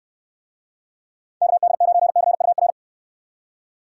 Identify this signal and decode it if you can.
Morse code, heard as a keyed tone, about 66 wpm, 700 Hz. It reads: KD9CUR